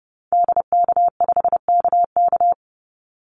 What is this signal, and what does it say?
Morse code, heard as a keyed tone, about 30 wpm, 700 Hz. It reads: DK5KK
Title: Call Signs Repeated 2x - 30wpm